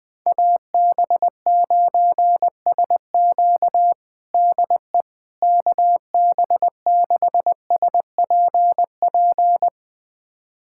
Morse code, heard as a keyed tone, about 21 wpm, 700 Hz. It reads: AB9SQ DE KB6SPP
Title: USA Callsign DE Callsign 20wpm